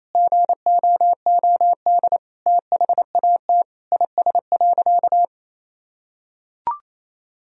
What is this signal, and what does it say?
Morse code, heard as a keyed tone, about 27 wpm, 700 Hz. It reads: GOOD THAT IS.
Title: Sentences from Top 1000 Words - Review Part 4 - 28wpm